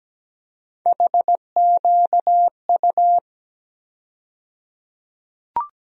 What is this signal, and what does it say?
Morse code, heard as a keyed tone, about 17 wpm, 700 Hz. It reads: HQU
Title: USA Call Sign Postfixes - 3 Letters 17wpm